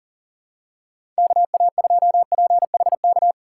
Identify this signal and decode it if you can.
Morse code, heard as a keyed tone, about 40 wpm, 700 Hz. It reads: KA2PHK
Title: USA Call Signs - 6 Characters - 4x Character Spacing - 40wpm